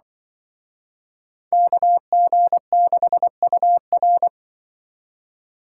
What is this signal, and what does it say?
Morse code, heard as a keyed tone, about 23 wpm, 700 Hz. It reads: KG6UR
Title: Speed Racing - USA Call Signs 20wpm